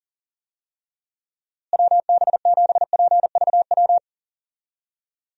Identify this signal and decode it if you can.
Morse code, heard as a keyed tone, about 40 wpm, 700 Hz. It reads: WB7PVW
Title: USA Call Signs - 6 Characters - Repeated 2x - 40wpm